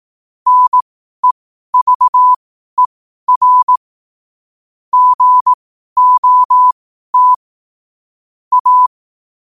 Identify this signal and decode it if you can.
Morse code, heard as a keyed tone, about 18 wpm, spaced as for 12 wpm, 1000 Hz. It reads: NEVER GOT A
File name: qotdfiles / 12wpm